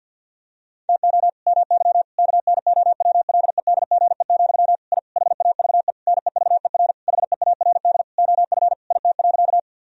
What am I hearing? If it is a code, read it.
Morse code, heard as a keyed tone, about 50 wpm, 700 Hz. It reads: TO MY KNOWLEDGE, I HAVE NEVER HEARD OF IT.
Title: Sentences from Top 1000 Words - Review Part 3 - 50wpm